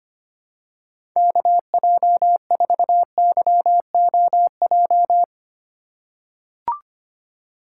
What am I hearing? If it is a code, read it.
Morse code, heard as a keyed tone, about 25 wpm, 700 Hz. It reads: KJ4YOJ
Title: USA Call Signs - 6 Characters - 3x Character Spacing - 25wpm